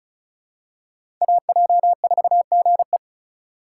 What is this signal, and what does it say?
Morse code, heard as a keyed tone, about 34 wpm, 700 Hz. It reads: AJ4GE
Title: USA Call Signs - 4x Character Spacing - 35wpm